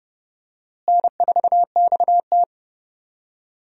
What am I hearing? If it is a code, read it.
Morse code, heard as a keyed tone, about 30 wpm, 700 Hz. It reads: N4XT
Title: USA Call Signs - 4 Characters - Repeated 2x - 30wpm